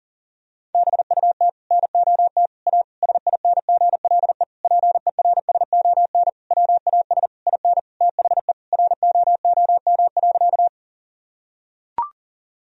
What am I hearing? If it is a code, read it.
Morse code, heard as a keyed tone, about 37 wpm, 700 Hz. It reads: BUT NOT A SINGLE PERSON WAS IN THE ROOM.
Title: Sentences from Top 1000 Words - Review Part 4 - 40wpm